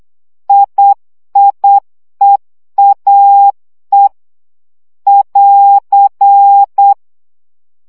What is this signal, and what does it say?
Morse code, heard as a keyed tone, about 8 wpm, 800 Hz. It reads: IIEAE <AR>
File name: CW gyakorlas e i a 7 baud